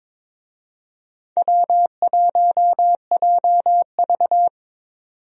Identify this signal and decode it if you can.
Morse code, heard as a keyed tone, about 22 wpm, 700 Hz. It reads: W1JV
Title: USA Call Signs - 4 Characters - 22wpm